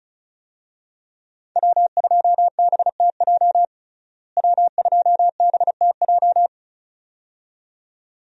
Morse code, heard as a keyed tone, about 34 wpm, 700 Hz. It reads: W2BTJ W2BTJ
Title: USA Call Signs - 5 Characters - Repeated 2x - 35wpm